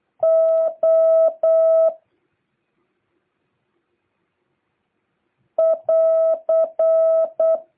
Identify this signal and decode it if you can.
Morse code, heard as a keyed tone, about 8 wpm, 650 Hz. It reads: O <AR>